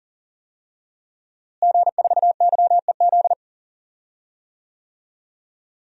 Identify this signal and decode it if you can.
Morse code, heard as a keyed tone, about 40 wpm, 700 Hz. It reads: G4YEZ
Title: DXCC-100 Call Signs - 5 Characters - Repeated 3x - 40wpm